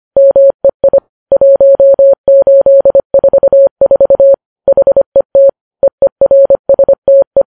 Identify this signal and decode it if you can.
Morse code, heard as a keyed tone, about 25 wpm, 550 Hz. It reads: MEI 1844 HET EERSTE